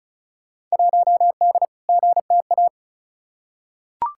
Morse code, heard as a keyed tone, about 34 wpm, 700 Hz. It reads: 1D GTA
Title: Field Day Exchanges - 35wpm